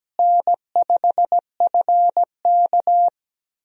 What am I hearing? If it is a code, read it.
Morse code, heard as a keyed tone, about 17 wpm, 700 Hz. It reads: N5FK
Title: USA Call Signs - 4 Characters - Repeated 2x - 17wpm